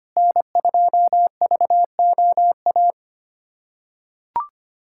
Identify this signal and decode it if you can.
Morse code, heard as a keyed tone, about 25 wpm, 700 Hz. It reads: N2VOA
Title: USA Callsign DE Callsign - 5 Characters - 25wpm